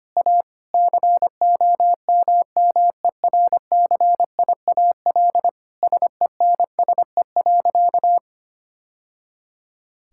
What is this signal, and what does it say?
Morse code, heard as a keyed tone, about 25 wpm, 700 Hz. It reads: A COMMERCIAL SENSE.
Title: Sentences from Top 1000 Words - Review Part 5 - 25wpm